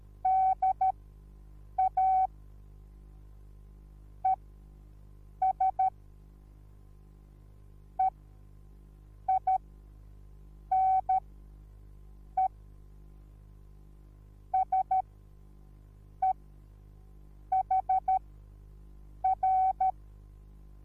Morse code, heard as a keyed tone, about 13 wpm, spaced as for 6 wpm, 760 Hz. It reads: DA ES EINE SEHR
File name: Onlinelehrgang / morsen / DJ4UF Morsen21